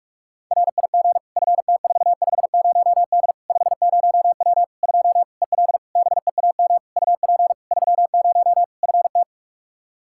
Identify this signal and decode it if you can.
Morse code, heard as a keyed tone, about 46 wpm, 700 Hz. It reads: RIG FT450D 50W 2 EL BEAM UP 30 FT
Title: Practice QSO - Exchange 2 - Info - 45wpm